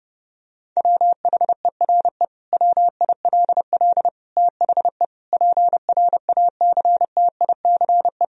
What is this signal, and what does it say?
Morse code, heard as a keyed tone, about 30 wpm, 700 Hz. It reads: WHERE WILL THE PRACTICE
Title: Sentences from Top 1000 Words - Review Part 4 - 30wpm